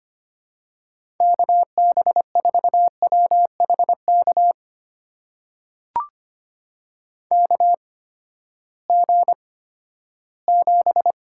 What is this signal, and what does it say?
Morse code, heard as a keyed tone, about 25 wpm, spaced as for 14 wpm, 700 Hz. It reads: KB4WHK KG7
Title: USA Call Signs - 6 Characters - 8x Character Spacing - 25wpm